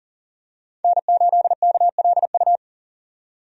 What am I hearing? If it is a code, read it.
Morse code, heard as a keyed tone, about 39 wpm, 700 Hz. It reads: N8KLU
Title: USA Call Signs - 5 Characters - 40wpm